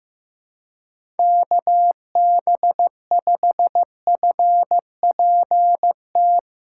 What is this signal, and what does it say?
Morse code, heard as a keyed tone, about 15 wpm, 700 Hz. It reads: KB5FPT
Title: USA Callsign DE Callsign - 6 Characters - 15wpm